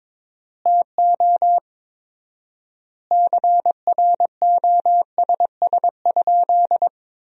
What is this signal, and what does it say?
Morse code, heard as a keyed tone, about 22 wpm, 700 Hz. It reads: TO CROSS?
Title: 4x Word-Spacing - Sentences from Top 300 Words - 22wpm